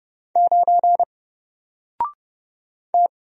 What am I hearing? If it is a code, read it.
Morse code, heard as a keyed tone, about 30 wpm, 700 Hz. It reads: 9 T
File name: Single-Letter-Number-30wpm